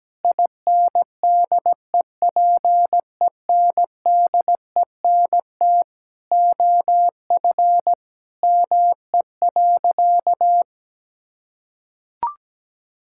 Morse code, heard as a keyed tone, about 17 wpm, 700 Hz. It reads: INDEPENDENT OF ME.
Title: Sentences from Top 1000 Words - Review Part 6 - 17wpm